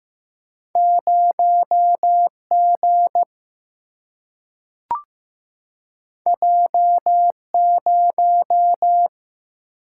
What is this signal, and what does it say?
Morse code, heard as a keyed tone, about 15 wpm, 700 Hz. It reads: 0 G J 0
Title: Single-Letter Single-Number 15wpm